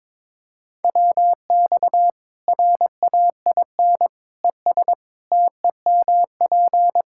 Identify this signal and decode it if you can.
Morse code, heard as a keyed tone, about 21 wpm, 700 Hz. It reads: WX RAIN ES TEMP
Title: Practice QSO - Exchange 2 - Info - 22wpm